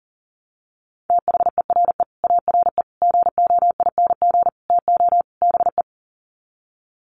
Morse code, heard as a keyed tone, about 40 wpm, 700 Hz. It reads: THERE ARE GOING TO BE
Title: Sets of 5 Words 40wpm